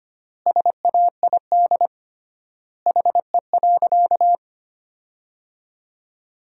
Morse code, heard as a keyed tone, about 25 wpm, 700 Hz. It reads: SAID HE.
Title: 3x Word-Spacing - Sentences from Top 100 Words - 25wpm